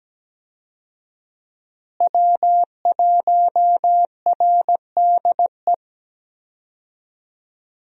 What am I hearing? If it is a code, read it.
Morse code, heard as a keyed tone, about 17 wpm, 700 Hz. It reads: W1RDE
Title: USA Call Signs - 5 Characters - Repeated 2x - 17wpm